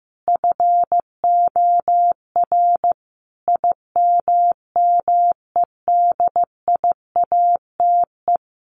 Morse code, heard as a keyed tone, about 15 wpm, 700 Hz. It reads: FOR IMMEDIATE
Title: Sentences from Top 1200 Words 15wpm